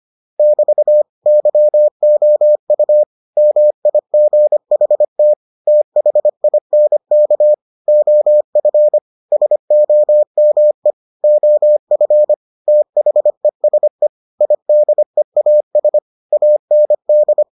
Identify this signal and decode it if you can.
Morse code, heard as a keyed tone, about 25 wpm, 600 Hz. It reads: <BT> YOU MIGHT THINK OF SOME OF THESE IDEAS AND